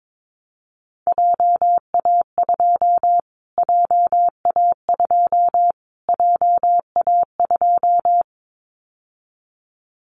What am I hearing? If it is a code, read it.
Morse code, heard as a keyed tone, about 22 wpm, 700 Hz. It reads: JA2 JA2 JA2
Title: Callsign Prefixes for Top 25 DXCC - Repeated 3x - 22wpm